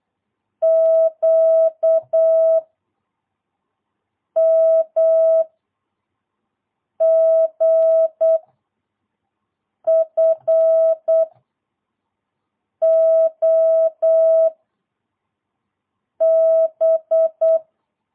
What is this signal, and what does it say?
Morse code, heard as a keyed tone, about 8 wpm, 650 Hz. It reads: Q M G F O B
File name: jh8cbh / 2022cw